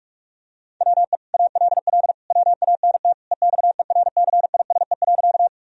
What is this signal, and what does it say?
Morse code, heard as a keyed tone, about 41 wpm, 700 Hz. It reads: WE ALL WANT EXERCISE.
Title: Sentences from Top 1000 Words - Review Part 4 - 45wpm